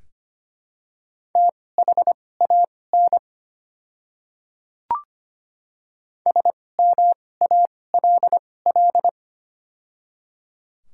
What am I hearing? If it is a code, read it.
Morse code, heard as a keyed tone, about 25 wpm, spaced as for 18 wpm, 700 Hz. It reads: THAN SMALL
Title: 2x Character Spacing - Top 500 Words - 25wpm